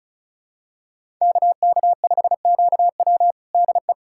Morse code, heard as a keyed tone, about 36 wpm, 700 Hz. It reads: KK5QW DE
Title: USA Callsign DE Callsign - 5 Characters - 35wpm